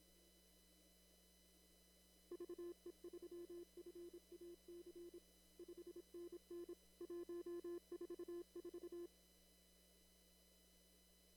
Morse code, heard as a keyed tone, about 26 wpm, 340 Hz. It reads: VE3FAC 5NN 144
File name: CW / kt5j wpx cw-op-k5pi